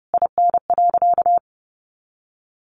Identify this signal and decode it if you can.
Morse code, heard as a keyed tone, about 30 wpm, 700 Hz. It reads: IN.
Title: Sentences from Top 400 Words 30wpm